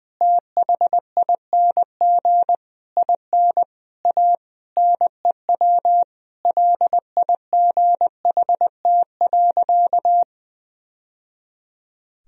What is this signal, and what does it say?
Morse code, heard as a keyed tone, about 20 wpm, 700 Hz. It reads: THING IN A NEW LIGHT.
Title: Sentences from Top 1000 Words - Review Part 1 - 20wpm